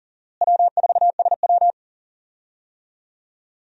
Morse code, heard as a keyed tone, about 40 wpm, 700 Hz. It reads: W4SW
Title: USA Call Signs - 4 Characters - Repeated 3x - 40wpm